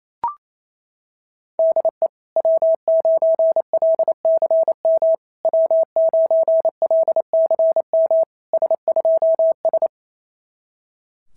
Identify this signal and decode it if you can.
Morse code, heard as a keyed tone, about 27 wpm, 660 Hz. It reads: DE W9LCM W9LCM S2S
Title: SOTA QSO Practice 28wpm